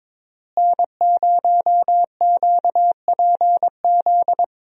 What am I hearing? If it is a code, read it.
Morse code, heard as a keyed tone, about 22 wpm, 700 Hz. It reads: N0QPZ
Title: USA Call Signs - 5 Characters - Repeated 3x - 22wpm